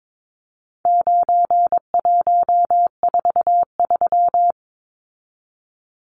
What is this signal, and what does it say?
Morse code, heard as a keyed tone, about 22 wpm, 700 Hz. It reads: 9143
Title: Numbers - 4-Digit 22wpm